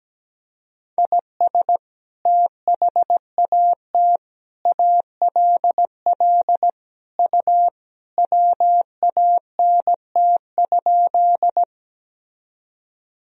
Morse code, heard as a keyed tone, about 17 wpm, 700 Hz. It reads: IS THAT ALL U WANT?
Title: Sentences from Top 200 Words and Abbreviations 17wpm